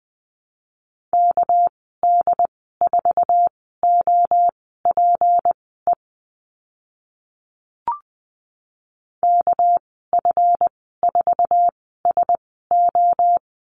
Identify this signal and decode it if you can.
Morse code, heard as a keyed tone, about 20 wpm, spaced as for 14 wpm, 700 Hz. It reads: KD4OPE KF4SO
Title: 213 Lesson 34g - Number 8 - Call Signs 20wpm